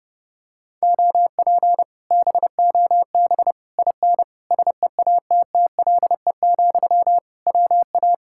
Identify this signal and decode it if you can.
Morse code, heard as a keyed tone, about 31 wpm, 700 Hz. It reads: OP BOB IN SEATTLE, WA